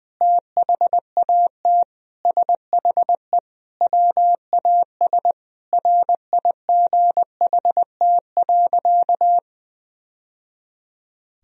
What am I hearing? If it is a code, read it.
Morse code, heard as a keyed tone, about 20 wpm, 700 Hz. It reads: THAT SHE WAS RIGHT.
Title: Sentences from Top 1000 Words - Review Part 1 - 20wpm